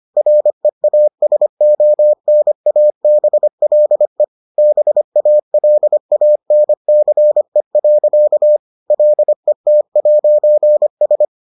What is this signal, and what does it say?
Morse code, heard as a keyed tone, about 25 wpm, 600 Hz. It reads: REASONABLE BALANCE. LET'S